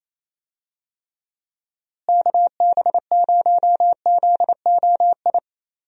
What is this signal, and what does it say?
Morse code, heard as a keyed tone, about 27 wpm, 700 Hz. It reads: KB0ZOI
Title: USA Call Signs - 6 Characters - Repeated 2x - 28wpm